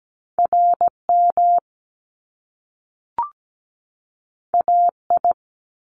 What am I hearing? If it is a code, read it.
Morse code, heard as a keyed tone, about 17 wpm, 700 Hz. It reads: RM AI
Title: Top Two Letter Combinations in Words 17wpm